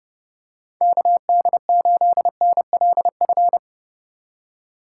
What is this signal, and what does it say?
Morse code, heard as a keyed tone, about 29 wpm, 700 Hz. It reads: KD8NLF